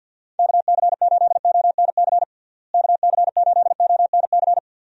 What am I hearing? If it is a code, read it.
Morse code, heard as a keyed tone, about 52 wpm, 700 Hz. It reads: KC8ONC KC8ONC
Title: USA Call Signs - 6 Characters - Repeated 3x - 50wpm